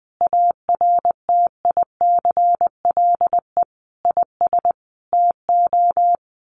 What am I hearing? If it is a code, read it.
Morse code, heard as a keyed tone, about 19 wpm, 700 Hz. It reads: ARTICLE IS TO
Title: Sets of 5 Words Encore2 20wpm